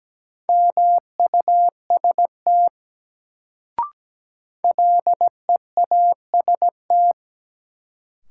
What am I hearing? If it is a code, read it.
Morse code, heard as a keyed tone, about 17 wpm, 700 Hz. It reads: MUST LEAST
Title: Sets of 2 Words - New Word Drill - 17wpm